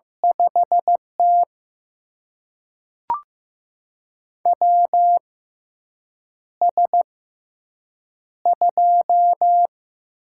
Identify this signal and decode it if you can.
Morse code, heard as a keyed tone, about 15 wpm, spaced as for 7 wpm, 700 Hz. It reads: 5T WS2
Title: USA Call Signs - 4 Characters - 6x Character Spacing - 15wpm